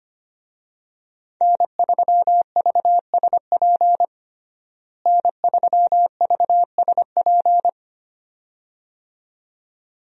Morse code, heard as a keyed tone, about 25 wpm, 700 Hz. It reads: N3VSP N3VSP
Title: USA Call Signs - 5 Characters - Repeated 2x - 25wpm